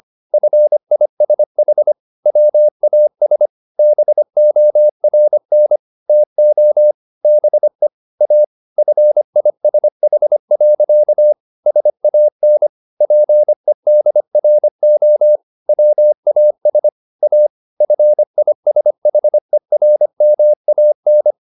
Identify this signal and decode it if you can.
Morse code, heard as a keyed tone, about 25 wpm, 600 Hz. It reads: FISH WAS BORN TO BE A FISH. SAN PEDRO WAS A FISHERMAN